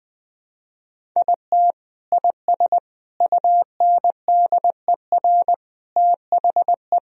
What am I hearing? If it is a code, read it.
Morse code, heard as a keyed tone, about 21 wpm, 700 Hz. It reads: IT IS UNDER THE
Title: Sentences from Top 1000 Words - Review Part 1 - 20wpm